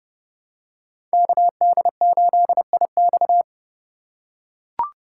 Morse code, heard as a keyed tone, about 30 wpm, 700 Hz. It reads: KD8IX
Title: USA Call Signs - 5 Characters - 4x Character Spacing - 30wpm